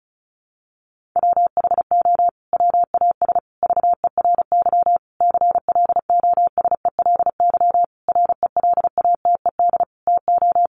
Morse code, heard as a keyed tone, about 34 wpm, 700 Hz. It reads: WHO WAS VERY CLOSELY RELATED TO
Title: Sentences from Top 1400 Words 35wpm